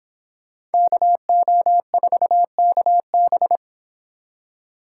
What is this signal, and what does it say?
Morse code, heard as a keyed tone, about 25 wpm, 700 Hz. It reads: KO4KB